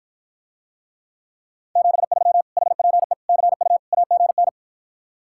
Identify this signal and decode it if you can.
Morse code, heard as a keyed tone, about 54 wpm, 700 Hz. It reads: 73 HPE CU AGN